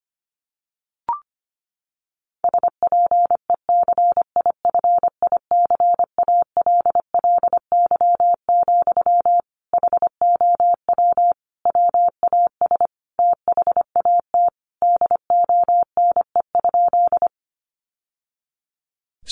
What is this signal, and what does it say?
Morse code, heard as a keyed tone, about 25 wpm, 700 Hz. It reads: SPECIFICALLY, HOW WAS THAT DONE?
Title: Sentences from Top 1100 Words 25wpm